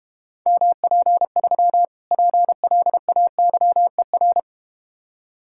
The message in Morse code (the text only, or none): MP3 PLAYER